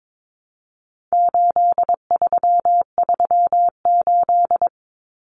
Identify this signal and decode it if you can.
Morse code, heard as a keyed tone, about 22 wpm, 700 Hz. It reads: 8338
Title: Numbers - 4-Digit 22wpm